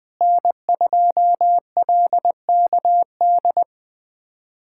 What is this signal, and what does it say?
Morse code, heard as a keyed tone, about 20 wpm, 700 Hz. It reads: N2LKD
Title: USA Callsign DE Callsign - 5 Characters - 20wpm